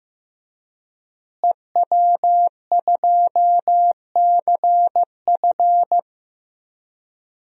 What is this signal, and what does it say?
Morse code, heard as a keyed tone, about 15 wpm, 700 Hz. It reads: EW2CF
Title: DXCC-100 Call Signs - 5 Characters - Repeated 3x - 15wpm